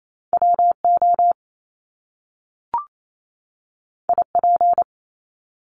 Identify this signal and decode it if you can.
Morse code, heard as a keyed tone, about 28 wpm, 700 Hz. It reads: WO IP
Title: Top Two Letter Combinations in Words 28wpm